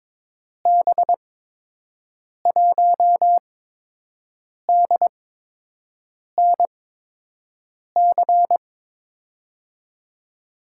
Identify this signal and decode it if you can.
Morse code, heard as a keyed tone, about 22 wpm, spaced as for 6 wpm, 700 Hz. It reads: B1DNC
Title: USA Call Signs - 6 Characters - 8x Character Spacing - 22wpm